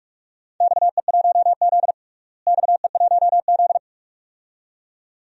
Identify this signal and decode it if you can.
Morse code, heard as a keyed tone, about 46 wpm, 700 Hz. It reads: XE1Z XE1Z
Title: DXCC-100 Call Signs - 4 Characters - Repeated 2x - 45wpm